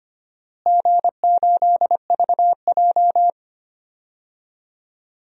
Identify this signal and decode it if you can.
Morse code, heard as a keyed tone, about 25 wpm, 700 Hz. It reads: G8VJ
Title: DXCC-100 Call Signs - 4 Characters - Repeated 3x - 25wpm